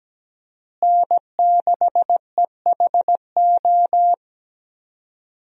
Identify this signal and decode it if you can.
Morse code, heard as a keyed tone, about 17 wpm, 700 Hz. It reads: N6EHO
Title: USA Call Signs - 5 Characters - Repeated 2x - 17wpm